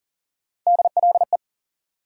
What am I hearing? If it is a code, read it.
Morse code, heard as a keyed tone, about 40 wpm, 700 Hz. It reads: DLE
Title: USA Call Sign Postfixes - 3 Letters 40wpm